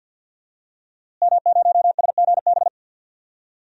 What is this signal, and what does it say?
Morse code, heard as a keyed tone, about 52 wpm, 700 Hz. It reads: M0SGB
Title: DXCC-100 Call Signs - 5 Characters - Repeated 3x - 50wpm